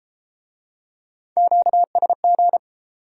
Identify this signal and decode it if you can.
Morse code, heard as a keyed tone, about 27 wpm, 700 Hz. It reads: QSG
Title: Speed-Racing Q-Codes - 22wpm